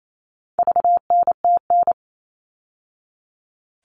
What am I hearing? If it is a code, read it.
Morse code, heard as a keyed tone, about 28 wpm, 700 Hz. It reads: VNTN
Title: All Cut Numbers - 4-Digit 28wpm